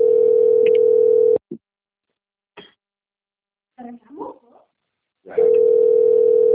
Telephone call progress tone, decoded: ringback tone

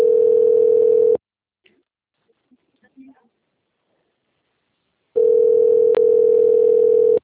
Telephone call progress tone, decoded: ringback tone

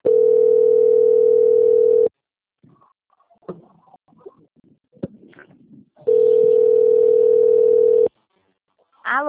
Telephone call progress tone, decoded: ringback tone

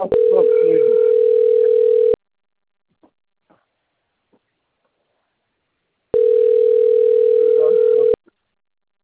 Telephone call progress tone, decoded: ringback tone

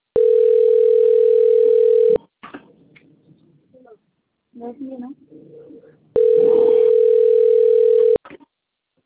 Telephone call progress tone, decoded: ringback tone

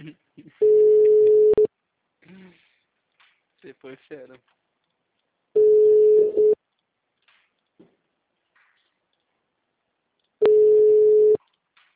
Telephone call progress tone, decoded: ringback tone

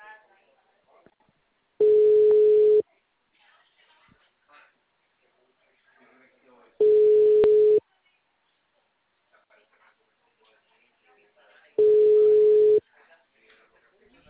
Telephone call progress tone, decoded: ringback tone

